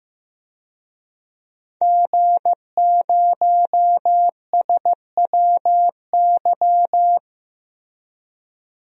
Morse code, heard as a keyed tone, about 16 wpm, 700 Hz. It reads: G0SWY
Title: DXCC-100 Call Signs - 5 Characters - Repeated 2x - 15wpm